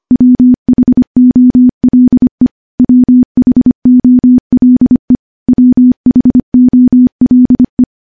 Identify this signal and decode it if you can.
Morse code, heard as a keyed tone, about 25 wpm, 250 Hz. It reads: WHOLE WHOLE WHOLE
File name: morse / morse-training-25WPM